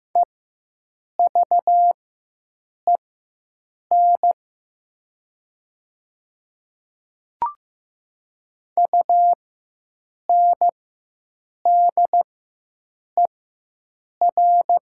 Morse code, heard as a keyed tone, about 15 wpm, spaced as for 7 wpm, 700 Hz. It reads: EVEN UNDER
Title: 4x Character Spacing - Top 200 Words - 15wpm